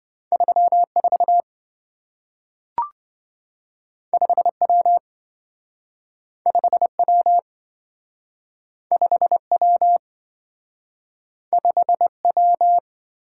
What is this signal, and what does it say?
Morse code, heard as a keyed tone, about 27 wpm, 700 Hz. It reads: 34 5W 5W 5W 5W